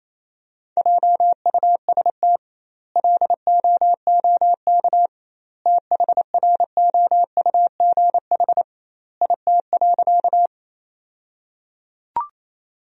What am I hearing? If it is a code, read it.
Morse code, heard as a keyed tone, about 27 wpm, 700 Hz. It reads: JUST LOOK THROUGH IT.
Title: 2x Word-Spacing - Sentences from Top 200 Words - 28wpm